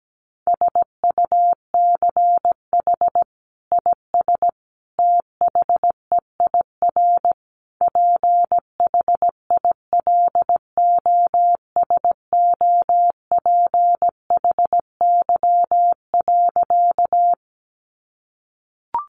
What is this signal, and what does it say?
Morse code, heard as a keyed tone, about 17 wpm, 700 Hz. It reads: SUCH IS THEIR PHILOSOPHY.
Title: Sentences from Top 900 Words 17wpm